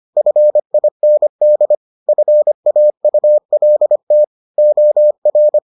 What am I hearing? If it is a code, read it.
Morse code, heard as a keyed tone, about 25 wpm, 600 Hz. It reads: FIND FAULT OR